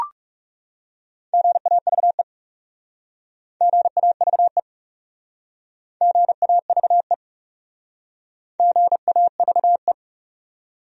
Morse code, heard as a keyed tone, about 40 wpm, 700 Hz. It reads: GAVE GAVE GAVE GAVE